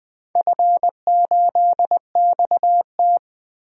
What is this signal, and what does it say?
Morse code, heard as a keyed tone, about 20 wpm, 700 Hz. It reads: F8XT
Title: DXCC-100 Call Signs - 4 Characters - 20wpm